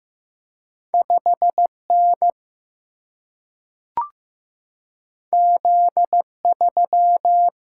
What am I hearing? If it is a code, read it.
Morse code, heard as a keyed tone, about 15 wpm, 700 Hz. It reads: 5N Z3
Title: Single-Letter Single-Number 15wpm